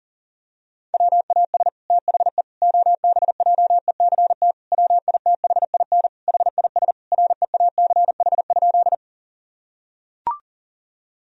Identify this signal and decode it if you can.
Morse code, heard as a keyed tone, about 40 wpm, 700 Hz. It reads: WAS THE OBJECT WITHIN HIS REACH?
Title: Sentences from Top 1000 Words - Review Part 4 - 40wpm